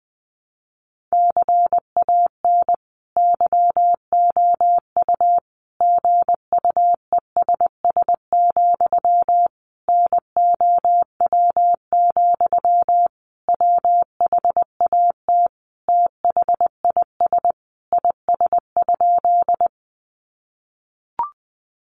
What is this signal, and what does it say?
Morse code, heard as a keyed tone, about 20 wpm, 700 Hz. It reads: CAN YOU GUESS, NOW, WHAT THIS IS?
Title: Sentences from Top 1500 Words 20wpm